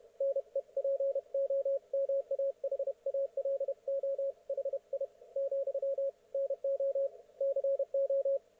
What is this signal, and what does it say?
Morse code, heard as a keyed tone, about 31 wpm, 530 Hz, with radio noise. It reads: NEPOMAHALO HI , NO CO
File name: Archiv / CUC107 / ok1fhi cul107